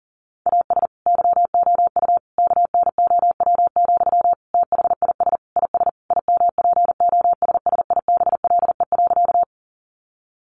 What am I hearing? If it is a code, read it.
Morse code, heard as a keyed tone, about 40 wpm, 700 Hz. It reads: AS YOU KNOW, THIS IS IMPOSSIBLE.
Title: Sentences from Top 1100 Words 40wpm